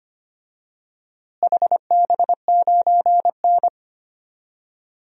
Morse code, heard as a keyed tone, about 25 wpm, 700 Hz. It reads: HB9N
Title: DXCC-100 Call Signs - 4 Characters - Repeated 3x - 25wpm